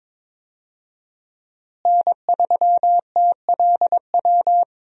Morse code, heard as a keyed tone, about 22 wpm, 700 Hz. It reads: N3TLW
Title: USA Call Signs - 5 Characters - Repeated 3x - 22wpm